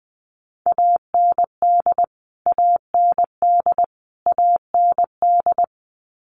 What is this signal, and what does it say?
Morse code, heard as a keyed tone, about 20 wpm, 700 Hz. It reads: AND AND AND
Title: Top 500 Words Repeated 20wpm